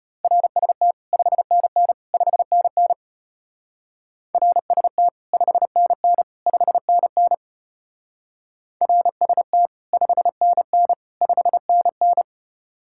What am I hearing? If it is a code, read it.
Morse code, heard as a keyed tone, about 32 wpm, 700 Hz. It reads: RST 5NN 5NN RST 5NN 5NN RST 5NN 5NN